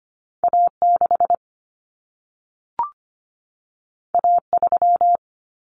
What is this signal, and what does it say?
Morse code, heard as a keyed tone, about 25 wpm, 700 Hz. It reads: A6 A3
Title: Common Cut Numbers - 2-Digit 25wpm